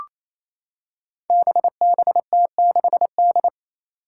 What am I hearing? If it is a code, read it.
Morse code, heard as a keyed tone, about 27 wpm, 700 Hz. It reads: BBT6D
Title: Five Challenging Letter-Number Combinations 28wpm